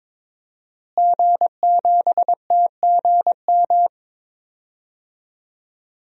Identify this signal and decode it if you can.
Morse code, heard as a keyed tone, about 21 wpm, 700 Hz. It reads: G7TGM
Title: DXCC-100 Call Signs - 5 Characters - Repeated 2x - 22wpm